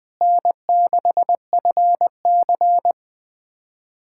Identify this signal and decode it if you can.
Morse code, heard as a keyed tone, about 20 wpm, 700 Hz. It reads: N6FC
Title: USA Callsign DE Callsign - 4 Characters - 20wpm